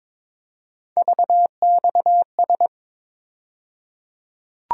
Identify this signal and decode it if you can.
Morse code, heard as a keyed tone, about 22 wpm, 700 Hz. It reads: VXS
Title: USA Call Sign Postfixes - 3 Letters 22wpm